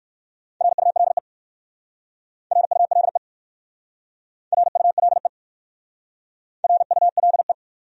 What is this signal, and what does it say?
Morse code, heard as a keyed tone, about 61 wpm, 700 Hz. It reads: RULE RULE RULE RULE